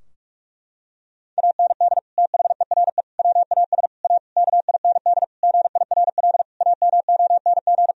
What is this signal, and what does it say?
Morse code, heard as a keyed tone, about 41 wpm, 700 Hz. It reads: AND THERE WAS A KIND GIRL AMONG
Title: Sentences from Top 1000 Words - Review Part 3 - 45wpm